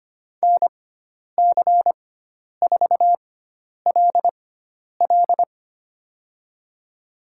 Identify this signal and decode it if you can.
Morse code, heard as a keyed tone, about 25 wpm, spaced as for 10 wpm, 700 Hz. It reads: NC4LL
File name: Lesson 16g - Letter c - Call Signs 25wpm